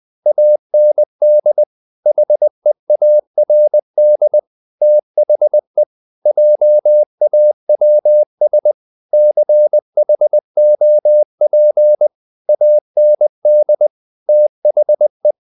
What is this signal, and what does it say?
Morse code, heard as a keyed tone, about 20 wpm, 600 Hz. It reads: AND HEARD THE JAWS CHOP AND THE
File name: the old man and the sea part 2 20wpm